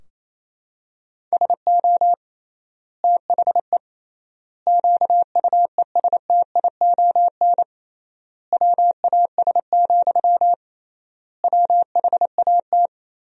Morse code, heard as a keyed tone, about 28 wpm, 700 Hz. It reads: SO THE QUESTION WAS, WHAT
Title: 3x Word-Spacing - Sentences from Top 400 Words - 28wpm